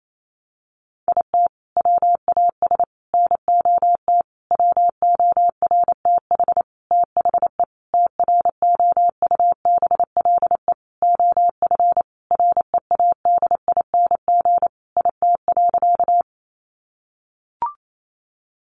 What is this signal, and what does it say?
Morse code, heard as a keyed tone, about 28 wpm, 700 Hz. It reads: IT WAS NOT WORTH THE TROUBLE OF READING IT.
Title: Sentences from Top 1200 Words 28wpm